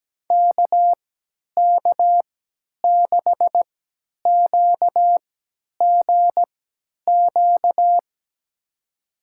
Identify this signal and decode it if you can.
Morse code, heard as a keyed tone, about 17 wpm, spaced as for 10 wpm, 700 Hz. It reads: KK6QGQ
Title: USA Call Signs - 6 Characters - 3x Character Spacing - 17wpm